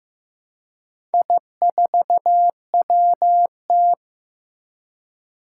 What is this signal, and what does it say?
Morse code, heard as a keyed tone, about 15 wpm, 700 Hz. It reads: I4WT
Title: Four Challenging Letter-Number Combinations 15wpm